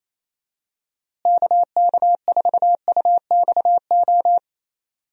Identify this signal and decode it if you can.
Morse code, heard as a keyed tone, about 28 wpm, 700 Hz. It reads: KK4UXO
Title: USA Call Signs - 6 Characters - Repeated 3x - 28wpm